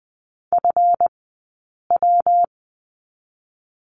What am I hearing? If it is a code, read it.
Morse code, heard as a keyed tone, about 20 wpm, 700 Hz. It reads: F W
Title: 183 Lesson 30d - Letter b - Two Characters 20wpm